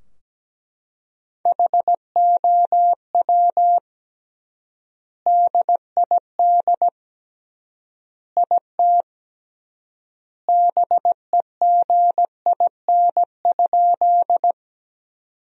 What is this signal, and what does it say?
Morse code, heard as a keyed tone, about 17 wpm, 700 Hz. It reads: HOW DID IT BEGIN?
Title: 3x Word-Spacing - Sentences from Top 300 Words - 17wpm